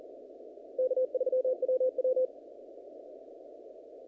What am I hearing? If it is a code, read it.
Morse code, heard as a keyed tone, about 41 wpm, 510 Hz. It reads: K3WW
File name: contests / naqp-cw-k5tr-op-k5pi- -1356